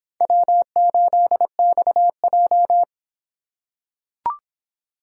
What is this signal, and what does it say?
Morse code, heard as a keyed tone, about 26 wpm, 700 Hz. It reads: W8XJ